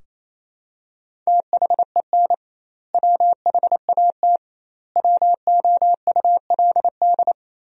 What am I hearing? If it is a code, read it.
Morse code, heard as a keyed tone, about 27 wpm, 700 Hz. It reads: THEN WHAT WOULD
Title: 2x Word-Spacing - Sentences from Top 400 Words - 28wpm